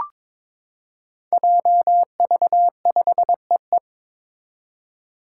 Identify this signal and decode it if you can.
Morse code, heard as a keyed tone, about 21 wpm, 700 Hz. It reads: JV5EE
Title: Five Challenging Letter-Number Combinations 22wpm